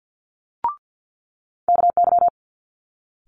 Morse code, heard as a keyed tone, about 50 wpm, 700 Hz. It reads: KY
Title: Multi-letter Phonemes Quick Fire 50wpm